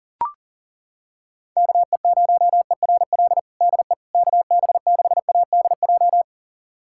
Morse code, heard as a keyed tone, about 40 wpm, 700 Hz. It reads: KE0ERL DE KB6ADJ
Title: USA Callsign DE Callsign 40wpm